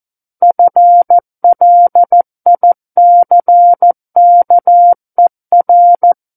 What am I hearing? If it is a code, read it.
Morse code, heard as a keyed tone, about 14 wpm, 700 Hz. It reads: FLICKER
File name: Book One / 05-Chapter 5 - 14 WPM